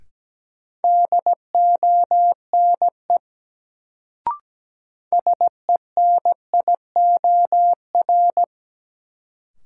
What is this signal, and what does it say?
Morse code, heard as a keyed tone, about 17 wpm, 700 Hz. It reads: DONE SENIOR
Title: Sets of 5 Words Encore - New Word Drill - 17wpm